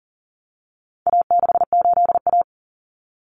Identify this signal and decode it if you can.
Morse code, heard as a keyed tone, about 40 wpm, 700 Hz. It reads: A68A
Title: Common Cut Numbers - 4-Digit 40wpm